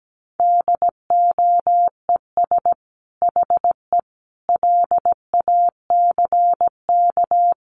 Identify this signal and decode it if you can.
Morse code, heard as a keyed tone, about 17 wpm, 700 Hz. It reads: DOES HE LACK
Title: Sentences from Top 1100 Words 17wpm